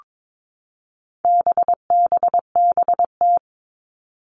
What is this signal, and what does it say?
Morse code, heard as a keyed tone, about 22 wpm, 700 Hz. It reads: BBBT
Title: All Cut Numbers - 4-Digit 22wpm